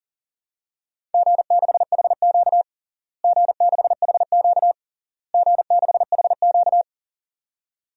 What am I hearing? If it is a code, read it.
Morse code, heard as a keyed tone, about 40 wpm, 700 Hz. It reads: G6HQ G6HQ G6HQ
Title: DXCC-100 Call Signs - 4 Characters - Repeated 3x - 40wpm